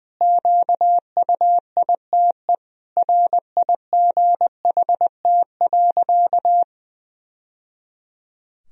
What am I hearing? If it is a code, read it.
Morse code, heard as a keyed tone, about 20 wpm, 700 Hz. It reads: QUITE RIGHT.
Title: Sentences from Top 1000 Words - Review Part 2 - 20wpm